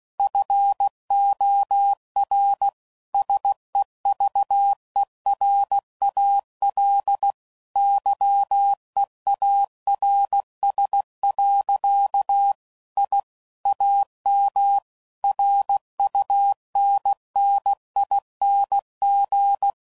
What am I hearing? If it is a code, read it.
Morse code, heard as a keyed tone, about 16 wpm, 800 Hz. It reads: FOR SEVERAL YEARS. I AM RUNNING
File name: CW Code Groups @15WPM / QSOs / kc7sac